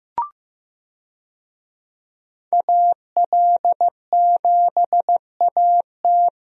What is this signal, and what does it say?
Morse code, heard as a keyed tone, about 15 wpm, 700 Hz. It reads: AL7AT
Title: USA Call Signs - 5 Characters - Repeated 3x - 15wpm